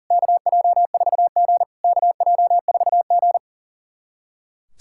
Morse code, heard as a keyed tone, about 40 wpm, 690 Hz. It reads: KJ4G KJ4G
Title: SOTA QSO Practice 40wpm